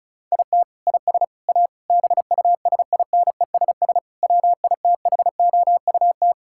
Morse code, heard as a keyed tone, about 34 wpm, 700 Hz. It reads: IT IS A BUSINESS WITHOUT
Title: Sentences from Top 1600 Words 35wpm